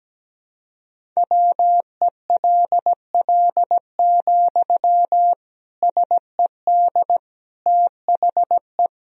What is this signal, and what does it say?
Morse code, heard as a keyed tone, about 17 wpm, 700 Hz. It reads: WELL, SED THE
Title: Sentences from Top 400 Words and Abbreviations 17wpm